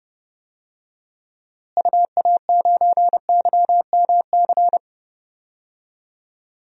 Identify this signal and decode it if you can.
Morse code, heard as a keyed tone, about 30 wpm, 700 Hz. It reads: UA9YMC
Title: DXCC-100 Call Signs - 6 Characters - Repeated 2x - 30wpm